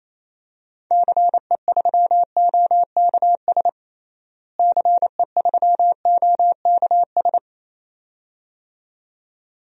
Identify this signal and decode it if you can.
Morse code, heard as a keyed tone, about 28 wpm, 700 Hz. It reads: CE3OKS CE3OKS
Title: DXCC-100 Call Signs - 6 Characters - Repeated 2x - 28wpm